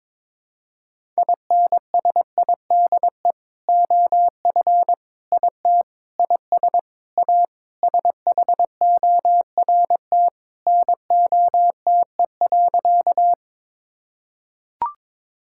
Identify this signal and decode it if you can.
Morse code, heard as a keyed tone, about 22 wpm, 700 Hz. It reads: INSIDE OF IT IS A SHORT NOTE.
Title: Sentences from Top 1000 Words - Review Part 6 - 22wpm